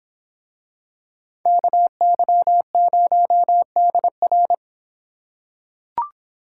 Morse code, heard as a keyed tone, about 26 wpm, 700 Hz. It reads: KY0DR